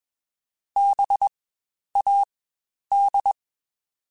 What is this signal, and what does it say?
Morse code, heard as a keyed tone, about 21 wpm, spaced as for 10 wpm, 800 Hz. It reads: BAD
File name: Spaced Alphas / Spaced Alphas mp3s / Bs sp @20